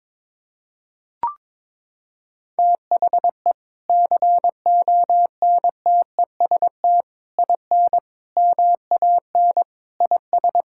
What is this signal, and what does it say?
Morse code, heard as a keyed tone, about 22 wpm, 700 Hz. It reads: THE CONTEST IN MAN IS
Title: Sentences from Top 1600 Words 22wpm